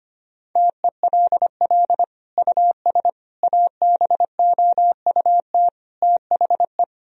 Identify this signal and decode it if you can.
Morse code, heard as a keyed tone, about 25 wpm, 700 Hz. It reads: TELL US ABOUT THE
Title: Sentences from Top 1000 Words - Review Part 4 - 25wpm